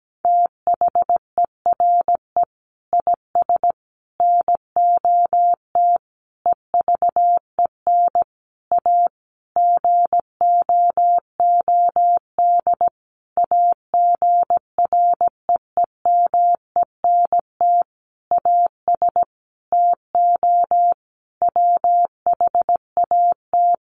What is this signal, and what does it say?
Morse code, heard as a keyed tone, about 17 wpm, 700 Hz. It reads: THERE IS NOT EVEN A GOOD AGREEMENT AS TO WHAT
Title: Sentences from Top 1500 Words 17wpm